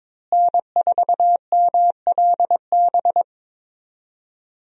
Morse code, heard as a keyed tone, about 22 wpm, 700 Hz. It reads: N4MLB
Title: USA Call Signs - 5 Characters - Repeated 3x - 22wpm